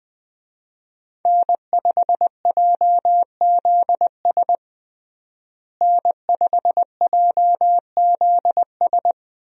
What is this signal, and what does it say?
Morse code, heard as a keyed tone, about 20 wpm, 700 Hz. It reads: N5JZS N5JZS
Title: USA Call Signs - 5 Characters - Repeated 2x - 20wpm